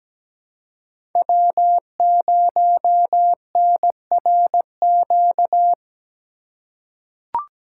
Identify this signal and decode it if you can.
Morse code, heard as a keyed tone, about 17 wpm, 700 Hz. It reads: W0NRQ
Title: USA Call Signs - 5 Characters - 4x Character Spacing - 17wpm